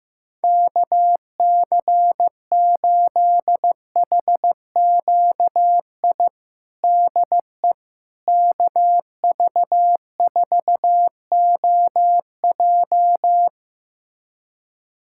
Morse code, heard as a keyed tone, about 15 wpm, 700 Hz. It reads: KC8HQI DE KV4OJ
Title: USA Callsign DE Callsign 15wpm